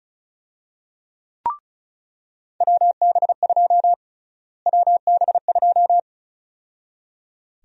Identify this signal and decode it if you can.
Morse code, heard as a keyed tone, about 34 wpm, 700 Hz. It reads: WB2 WB2
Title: USA Call Sign Prefixes - Repeated 2x - 35wpm